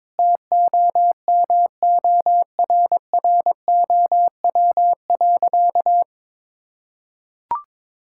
Morse code, heard as a keyed tone, about 22 wpm, 700 Hz. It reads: TOMORROW.
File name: practice / 22wpm / Sentences with 4-Word Phrases - 22wpm